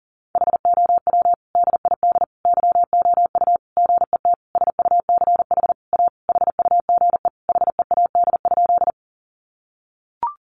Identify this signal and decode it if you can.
Morse code, heard as a keyed tone, about 40 wpm, 700 Hz. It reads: HOW DID YOU GET SUCH A HUGE HEAD?
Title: Sentences from Top 1100 Words 40wpm